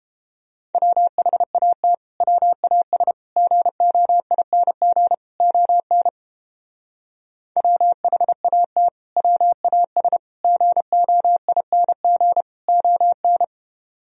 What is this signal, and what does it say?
Morse code, heard as a keyed tone, about 29 wpm, 700 Hz. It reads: WHAT WAS GOING ON WHAT WAS GOING ON